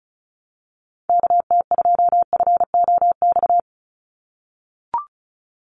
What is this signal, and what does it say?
Morse code, heard as a keyed tone, about 34 wpm, 700 Hz. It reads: KT2FOX
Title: USA Call Signs 35wpm